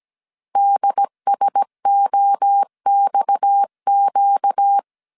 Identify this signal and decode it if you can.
Morse code, heard as a keyed tone, about 17 wpm, 800 Hz. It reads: DSOXQ
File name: CW / DET / 1 6 1500C S A 16wpm 2635